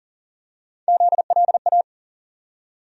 Morse code, heard as a keyed tone, about 40 wpm, 700 Hz. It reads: ZLA
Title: USA Call Sign Postfixes - 3 Letters 40wpm